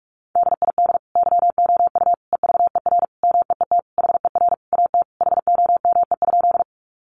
Morse code, heard as a keyed tone, about 44 wpm, 700 Hz. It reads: DID YOU EVER MEET HER AT HOME?
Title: Sentences from Top 1100 Words 45wpm